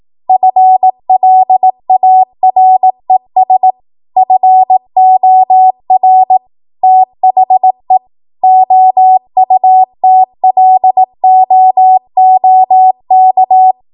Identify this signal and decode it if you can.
Morse code, heard as a keyed tone, about 18 wpm, 750 Hz. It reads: FLARES FOR THE OUTLOOK
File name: Morse / 18WPM